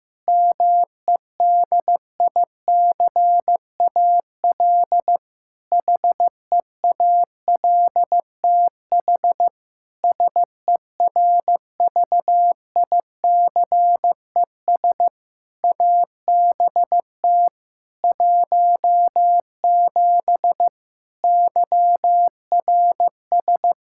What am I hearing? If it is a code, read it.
Morse code, heard as a keyed tone, about 15 wpm, 700 Hz. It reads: MEDICAL HEALTH SERVICES ABT 17 YRS
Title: K1USN SST QSO Practice 15wpm